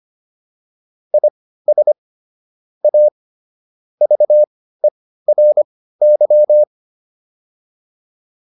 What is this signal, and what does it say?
Morse code, heard as a keyed tone, about 25 wpm, spaced as for 15 wpm, 600 Hz. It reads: IS A VERY